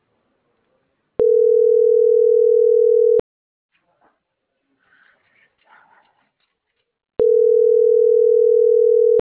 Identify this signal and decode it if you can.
Ringback tone (North America), 440 + 480 Hz.